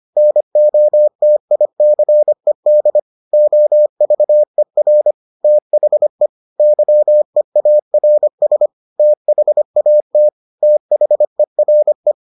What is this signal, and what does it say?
Morse code, heard as a keyed tone, about 25 wpm, 600 Hz. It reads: NOTICED OVER THE YEARS THAT THERE